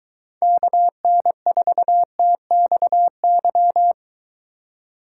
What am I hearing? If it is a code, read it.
Morse code, heard as a keyed tone, about 23 wpm, 700 Hz. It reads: KN4TXY